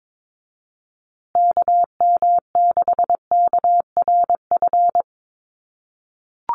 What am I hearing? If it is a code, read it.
Morse code, heard as a keyed tone, about 22 wpm, 690 Hz. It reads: KM6KRF
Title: USA Call Signs Repeated 3x 22wpm